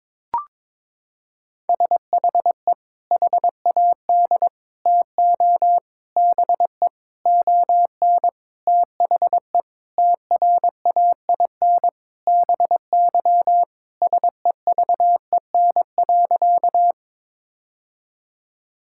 Morse code, heard as a keyed tone, about 22 wpm, 700 Hz. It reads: SHE HAD TO BE ON THE TRAIN BY SEVEN.
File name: practice / 22wpm / Sentences with 4-Word Phrases - 22wpm